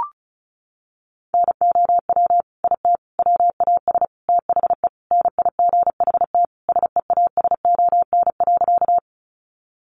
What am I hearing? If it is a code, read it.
Morse code, heard as a keyed tone, about 34 wpm, 700 Hz. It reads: NOW IT WAS THE NIGHT SEASON.
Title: Sentences from Top 1400 Words 35wpm